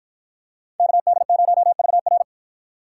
Morse code, heard as a keyed tone, about 51 wpm, 700 Hz. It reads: KD0VR